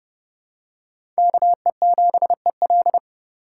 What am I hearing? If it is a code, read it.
Morse code, heard as a keyed tone, about 29 wpm, 700 Hz. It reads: KE7EL
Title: USA Call Signs - 5 Characters - 3x Character Spacing - 30wpm